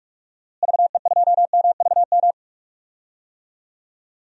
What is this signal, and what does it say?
Morse code, heard as a keyed tone, about 46 wpm, 700 Hz. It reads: VE2MVM
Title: DXCC-100 Call Signs - 6 Characters - Repeated 2x - 45wpm